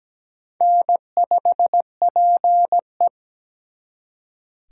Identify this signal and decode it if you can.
Morse code, heard as a keyed tone, about 20 wpm, 700 Hz. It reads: N5PE
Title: Speed Racing - USA Call Signs 17wpm